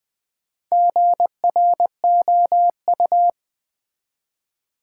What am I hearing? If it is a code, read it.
Morse code, heard as a keyed tone, about 20 wpm, 700 Hz. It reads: GROU
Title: Build A Word - 1 Letter At A Time - 3 to 5-Letter Words - 20wpm